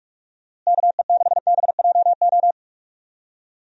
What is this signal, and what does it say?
Morse code, heard as a keyed tone, about 41 wpm, 700 Hz. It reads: KE6BJO
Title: USA Call Signs - 6 Characters - Repeated 3x - 45wpm